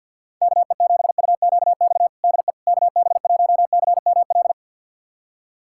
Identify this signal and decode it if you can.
Morse code, heard as a keyed tone, about 52 wpm, 700 Hz. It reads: KE7UQX DE KB1CML
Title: USA Callsign DE Callsign 50wpm